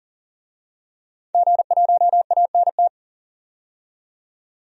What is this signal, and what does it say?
Morse code, heard as a keyed tone, about 40 wpm, 700 Hz. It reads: G1ANT
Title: DXCC-100 Call Signs - 5 Characters - 40wpm